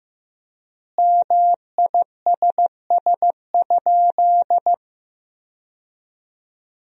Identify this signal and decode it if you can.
Morse code, heard as a keyed tone, about 15 wpm, 700 Hz. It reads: MISS?
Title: 3x Word-Spacing - Sentences from Top 500 Words - 15wpm